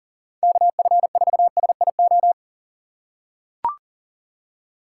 Morse code, heard as a keyed tone, about 37 wpm, 700 Hz. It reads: KF4SIO
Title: USA Call Signs - 6 Characters - 2x Character Spacing - 40wpm